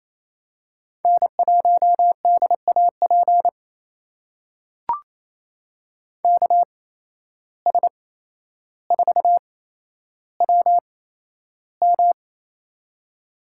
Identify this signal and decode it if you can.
Morse code, heard as a keyed tone, about 28 wpm, spaced as for 12 wpm, 700 Hz. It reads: N1DAP KS4WM
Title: USA Call Signs - 5 Characters - 8x Character Spacing - 28wpm